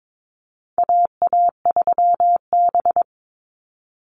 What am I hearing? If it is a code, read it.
Morse code, heard as a keyed tone, about 22 wpm, 700 Hz. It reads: AA3B
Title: Call Signs Repeated 3x - 22wpm